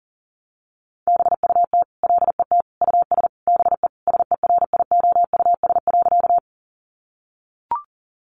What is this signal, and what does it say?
Morse code, heard as a keyed tone, about 40 wpm, 700 Hz. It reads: BUT LET US BE SERIOUS.
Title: Sentences from Top 1100 Words 40wpm